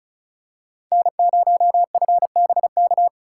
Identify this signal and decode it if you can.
Morse code, heard as a keyed tone, about 36 wpm, 700 Hz. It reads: N0FBK
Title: USA Call Signs - 5 Characters - 8x Character Spacing - 35wpm